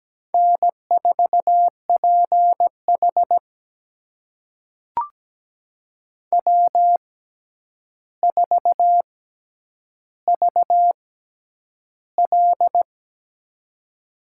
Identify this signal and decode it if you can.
Morse code, heard as a keyed tone, about 17 wpm, spaced as for 9 wpm, 700 Hz. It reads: N4PH W4VL
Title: USA Call Signs - 4 Characters - 6x Character Spacing - 17wpm